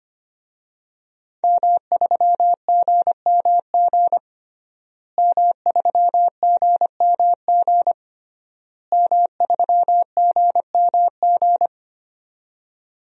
Morse code, heard as a keyed tone, about 25 wpm, 700 Hz. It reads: M3GMG M3GMG M3GMG
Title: DXCC-100 Call Signs - 5 Characters - Repeated 3x - 25wpm